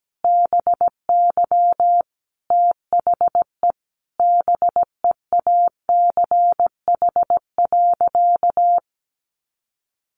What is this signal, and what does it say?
Morse code, heard as a keyed tone, about 17 wpm, 700 Hz. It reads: BY THE BEACH.
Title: Sentences from Top 1500 Words 17wpm